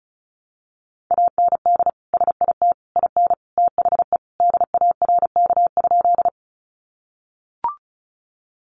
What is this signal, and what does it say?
Morse code, heard as a keyed tone, about 34 wpm, 700 Hz. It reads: AND SIT IN THE DARK?
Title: Sentences from Top 500 Words 35wpm